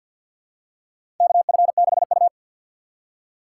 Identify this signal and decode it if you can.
Morse code, heard as a keyed tone, about 53 wpm, 700 Hz. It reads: KF6U
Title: USA Call Signs - 6x Character Spacing - 50wpm